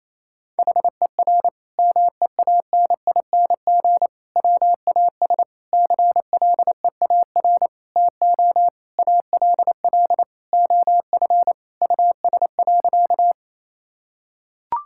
Morse code, heard as a keyed tone, about 28 wpm, 700 Hz. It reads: HER MEANING WAS CLEAR TO ALL OF US.
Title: Sentences from Top 1000 Words - Review Part 6 - 28wpm